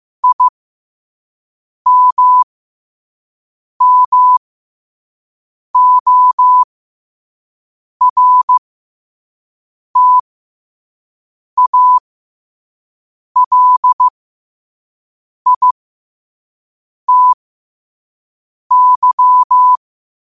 Morse code, heard as a keyed tone, about 15 wpm, spaced as for 5 wpm, 1000 Hz. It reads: IMMORTALITY
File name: qotdfiles / 5wpm